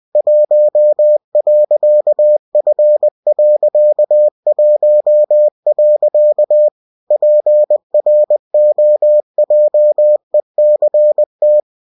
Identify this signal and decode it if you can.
Morse code, heard as a keyed tone, about 20 wpm, 600 Hz. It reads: 1.F.1. PROJECT